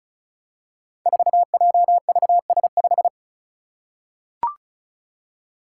Morse code, heard as a keyed tone, about 36 wpm, 700 Hz. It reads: 4JVS5
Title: Five Challenging Letter-Number Combinations 35wpm